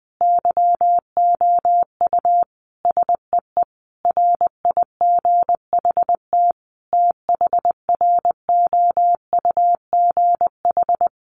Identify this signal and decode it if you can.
Morse code, heard as a keyed tone, about 20 wpm, 700 Hz. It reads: YOU SEE RIGHT THROUGH
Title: Sentences from Top 200 Words 20wpm